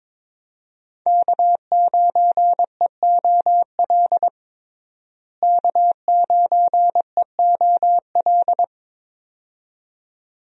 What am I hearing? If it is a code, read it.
Morse code, heard as a keyed tone, about 21 wpm, 700 Hz. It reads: K9EOL K9EOL
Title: USA Call Signs - 5 Characters - Repeated 2x - 22wpm